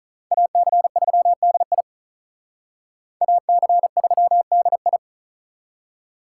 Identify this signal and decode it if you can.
Morse code, heard as a keyed tone, about 45 wpm, 700 Hz. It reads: AC3DI AC3DI